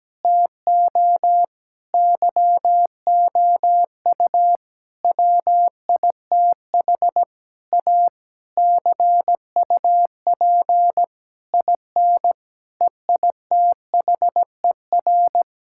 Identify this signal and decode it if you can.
Morse code, heard as a keyed tone, about 17 wpm, 700 Hz. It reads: TO YOU WITH A CUP IN EITHER
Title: Sentences from Top 1600 Words 17wpm